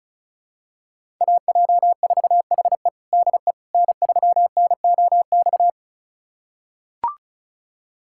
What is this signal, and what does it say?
Morse code, heard as a keyed tone, about 36 wpm, 700 Hz. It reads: AJ4HE DE N3NOX
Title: USA Callsign DE Callsign - 5 Characters - 35wpm